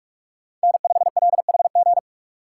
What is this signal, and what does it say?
Morse code, heard as a keyed tone, about 46 wpm, 700 Hz. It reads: N5LHG
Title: USA Call Signs - 5 Characters - 2x Character Spacing - 45wpm